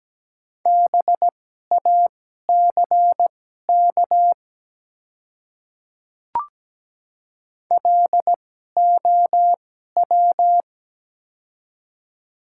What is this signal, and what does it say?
Morse code, heard as a keyed tone, about 17 wpm, spaced as for 12 wpm, 700 Hz. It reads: BACK LOW
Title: 2x Character Spacing - Top 300 Words - 17wpm